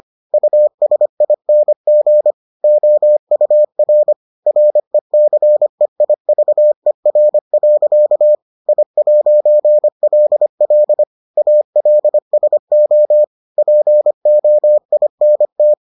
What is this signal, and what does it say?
Morse code, heard as a keyed tone, about 25 wpm, 600 Hz. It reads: USING OUR RECEIVER. I'LL ALSO POINT